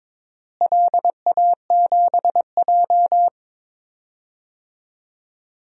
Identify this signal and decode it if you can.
Morse code, heard as a keyed tone, about 22 wpm, 700 Hz. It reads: LA7J
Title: DXCC-100 Call Signs - 4 Characters - Repeated 3x - 22wpm